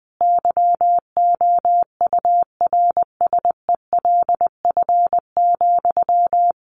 Morse code, heard as a keyed tone, about 20 wpm, 700 Hz. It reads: YOURSELF,
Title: Sentences from Top 1200 Words 20wpm